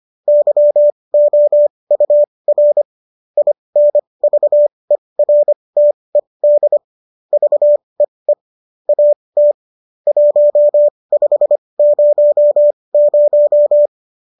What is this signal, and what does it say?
Morse code, heard as a keyed tone, about 25 wpm, spaced as for 20 wpm, 590 Hz. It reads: YOUR INVERTED VEE AT 1500